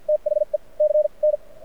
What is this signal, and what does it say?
Morse code, heard as a keyed tone, about 53 wpm, spaced as for 44 wpm, 600 Hz. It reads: THE C N
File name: archived / CRSnet